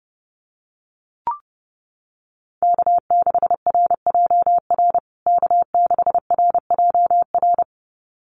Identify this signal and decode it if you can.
Morse code, heard as a keyed tone, about 30 wpm, 700 Hz. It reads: K6RJR K6RJR
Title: USA Call Signs Repeated 2x 30wpm